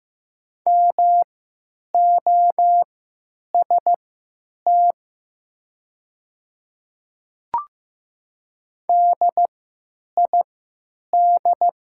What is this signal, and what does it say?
Morse code, heard as a keyed tone, about 15 wpm, spaced as for 9 wpm, 700 Hz. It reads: MOST DID
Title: 3x Character Spacing - Top 200 Words - 15wpm